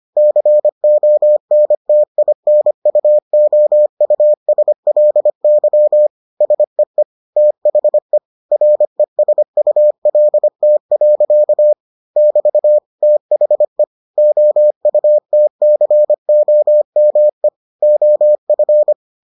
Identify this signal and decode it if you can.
Morse code, heard as a keyed tone, about 25 wpm, 600 Hz. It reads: CONTINUOUSLY SEE THE RESULT. <BT> THE OUTCOME OF